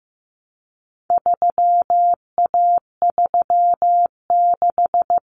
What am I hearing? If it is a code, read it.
Morse code, heard as a keyed tone, about 15 wpm, 700 Hz. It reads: 3A36
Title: Common Cut Numbers - 4-Digit 15wpm